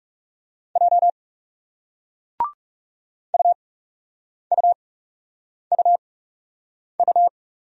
Morse code, heard as a keyed tone, about 40 wpm, spaced as for 8 wpm, 700 Hz. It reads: J UUUU